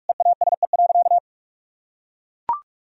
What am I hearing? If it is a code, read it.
Morse code, heard as a keyed tone, about 44 wpm, 700 Hz. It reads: EASE.
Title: Sentences from Top 1000 Words - Review Part 1 - 45wpm